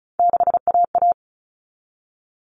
Morse code, heard as a keyed tone, about 34 wpm, 700 Hz. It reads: 6AA
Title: All Cut Numbers - 3-Digit 35wpm